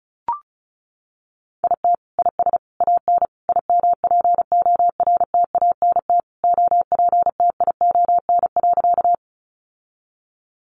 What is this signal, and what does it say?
Morse code, heard as a keyed tone, about 34 wpm, 700 Hz. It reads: IT IS AN IMPORTANT OPTION.
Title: Sentences from Top 1300 Words 35wpm